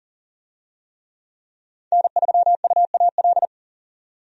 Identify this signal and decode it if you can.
Morse code, heard as a keyed tone, about 40 wpm, 700 Hz. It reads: N3UAL
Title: USA Call Signs - 5 Characters - 40wpm